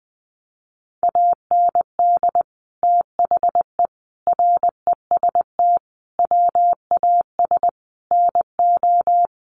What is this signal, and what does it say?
Morse code, heard as a keyed tone, about 20 wpm, 700 Hz. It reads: AND THE REST WAS NO
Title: Sentences from Top 1000 Words 20wpm